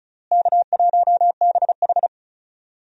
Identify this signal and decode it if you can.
Morse code, heard as a keyed tone, about 36 wpm, 700 Hz. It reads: K1BH
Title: USA Call Signs - 4 Characters - Repeated 2x - 35wpm